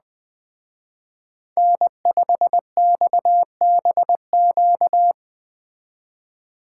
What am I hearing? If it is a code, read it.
Morse code, heard as a keyed tone, about 20 wpm, 700 Hz. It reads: N5XBQ
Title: Speed Racing - USA Call Signs 17wpm